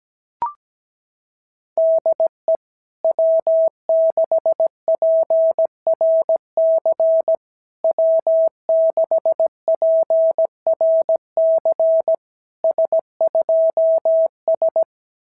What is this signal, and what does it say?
Morse code, heard as a keyed tone, about 17 wpm, 650 Hz. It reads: DE W6PRC W6PRC S2S
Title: SOTA QSO Practice 17wpm